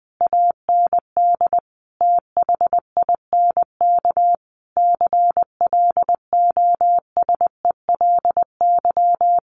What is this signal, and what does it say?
Morse code, heard as a keyed tone, about 20 wpm, 700 Hz. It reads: AND THINK CLOSELY
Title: Sentences from Top 1400 Words 20wpm